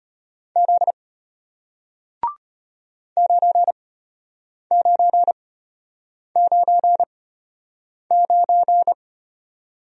Morse code, heard as a keyed tone, about 33 wpm, spaced as for 8 wpm, 700 Hz. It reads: Z 9999